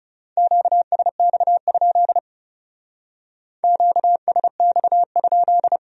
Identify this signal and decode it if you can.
Morse code, heard as a keyed tone, about 38 wpm, 700 Hz. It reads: QSX? QSX?